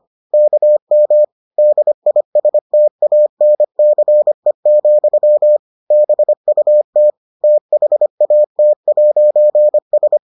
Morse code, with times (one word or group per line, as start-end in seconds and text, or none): 0.3-1.2 KM
1.6-5.6 DISTANCE,
5.9-7.1 BUT
7.4-10.2 THAT'S